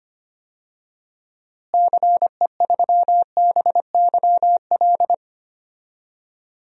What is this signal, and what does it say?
Morse code, heard as a keyed tone, about 25 wpm, 700 Hz. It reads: CE3BYL
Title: DXCC-100 Call Signs - 6 Characters - Repeated 3x - 25wpm